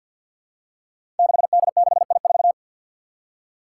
Morse code, heard as a keyed tone, about 47 wpm, 700 Hz. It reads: 6D6I4
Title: Five Challenging Letter-Number Combinations 50wpm